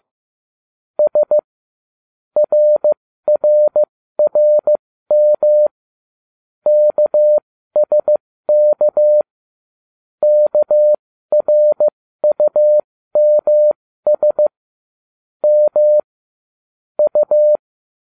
Morse code, heard as a keyed tone, about 15 wpm, spaced as for 13 wpm, 600 Hz. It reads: S RRRM KSK KRUMS M U